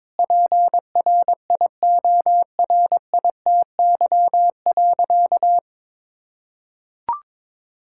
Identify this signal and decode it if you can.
Morse code, heard as a keyed tone, about 22 wpm, 700 Hz. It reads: PRIORITY.
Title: Sentences from Top 1600 Words 22wpm